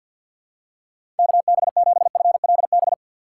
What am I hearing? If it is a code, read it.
Morse code, heard as a keyed tone, about 50 wpm, 700 Hz. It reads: KB7FLB
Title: USA Call Signs - 6 Characters - 6x Character Spacing - 50wpm